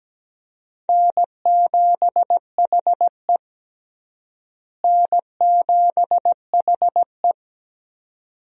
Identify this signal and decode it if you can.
Morse code, heard as a keyed tone, about 17 wpm, 700 Hz. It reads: N7HE N7HE
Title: USA Call Signs - 4 Characters - Repeated 2x - 17wpm